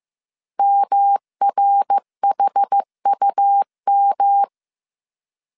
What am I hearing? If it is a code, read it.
Morse code, heard as a keyed tone, about 15 wpm, 800 Hz. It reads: MRHUM
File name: CW / DET / 1 6 1500C S B 14wpm 2958